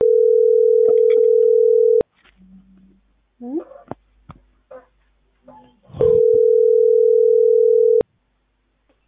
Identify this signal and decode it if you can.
Ringback tone (North America), 440 + 480 Hz.